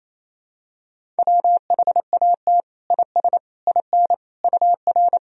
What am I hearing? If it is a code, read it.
Morse code, heard as a keyed tone, about 28 wpm, 700 Hz. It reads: WHAT IS IN UR
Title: Sentences from Top 200 Words and Abbreviations 28wpm